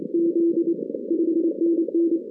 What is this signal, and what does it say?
Morse code, heard as a keyed tone, about 26 wpm, 340 Hz. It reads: Z 5NN
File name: 2016 arrl dx cw / 2016-ARRL-DX-CW-K5TR- - 0-41